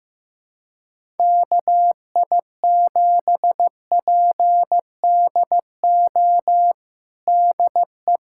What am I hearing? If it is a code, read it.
Morse code, heard as a keyed tone, about 15 wpm, 700 Hz. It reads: KI7PDO DE
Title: USA Callsign DE Callsign - 6 Characters - 15wpm